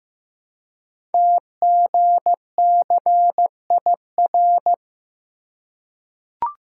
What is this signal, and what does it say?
Morse code, heard as a keyed tone, about 15 wpm, 700 Hz. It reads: TGCIR
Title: Code Groups - 5 Letters - 15wpm